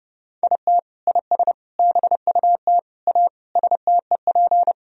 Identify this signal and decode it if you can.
Morse code, heard as a keyed tone, about 30 wpm, 700 Hz. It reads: IT IS BUT A STEP
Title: Sentences from Top 1000 Words - Review Part 2 - 30wpm